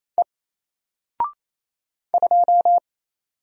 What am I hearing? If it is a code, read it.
Morse code, heard as a keyed tone, about 28 wpm, 700 Hz. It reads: E 2
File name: Single-Letter-Number-28wpm